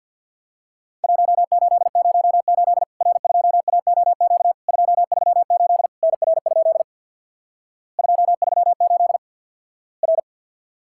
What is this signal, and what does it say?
Morse code, heard as a keyed tone, about 50 wpm, 700 Hz. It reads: 1808 R2AOQ 238 NR? 238 R
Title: Worked All Europe DX Contest - QTC Exchange with Repeats - 50wpm